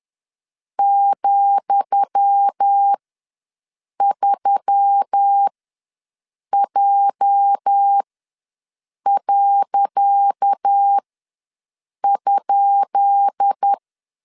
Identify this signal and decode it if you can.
Morse code, heard as a keyed tone, about 10 wpm, spaced as for 6 wpm, 800 Hz. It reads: ,3J.?
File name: LEZIONI / 10 SIGNS / 10D COMA ws 10wpm 4314